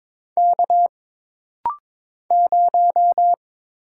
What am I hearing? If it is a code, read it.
Morse code, heard as a keyed tone, about 22 wpm, 700 Hz. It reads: K 0
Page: This rapid-fire timing is geared to push your ICR (Instant Character Recognition) skills.